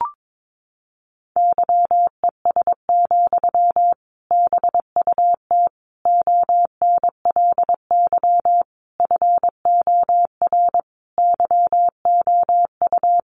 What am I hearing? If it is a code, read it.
Morse code, heard as a keyed tone, about 22 wpm, 700 Hz. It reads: YES, BUT ONLY FOR YOU
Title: Sentences from Top 500 Words 22wpm